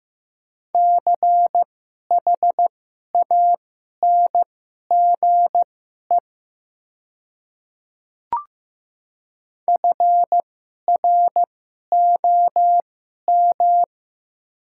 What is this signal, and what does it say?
Morse code, heard as a keyed tone, about 15 wpm, spaced as for 11 wpm, 700 Hz. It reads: CHANGE FROM
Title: 2x Character Spacing - Top 200 Words - 15wpm